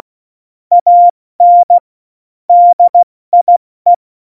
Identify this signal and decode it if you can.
Morse code, heard as a keyed tone, about 16 wpm, 700 Hz. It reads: AN DIE
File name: Morsekurs / Witze / Witz 26. Januar 2021, 70 F80